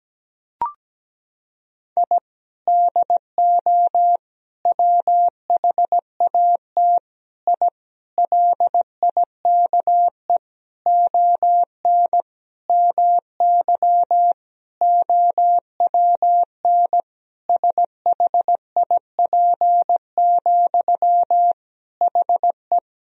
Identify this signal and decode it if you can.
Morse code, heard as a keyed tone, about 17 wpm, 700 Hz. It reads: I DO WHAT I LIKE ON MY OWN SHIP, HE
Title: Sentences from Top 1000 Words - Review Part 2 - 17wpm